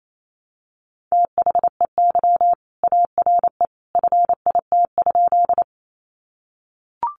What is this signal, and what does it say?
Morse code, heard as a keyed tone, about 28 wpm, 700 Hz. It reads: THEY ARE FIT?
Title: Sentences from Top 800 Words 28wpm